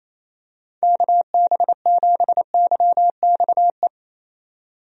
Morse code, heard as a keyed tone, about 28 wpm, 700 Hz. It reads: KB7YXE
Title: USA Call Signs - 6 Characters - 28wpm